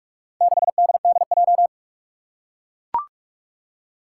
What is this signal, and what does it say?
Morse code, heard as a keyed tone, about 44 wpm, 700 Hz. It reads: 6DDJ
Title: Four Challenging Letter-Number Combinations 45wpm